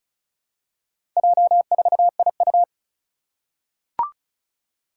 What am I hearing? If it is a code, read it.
Morse code, heard as a keyed tone, about 36 wpm, 700 Hz. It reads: J4IU
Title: Four Challenging Letter-Number Combinations 35wpm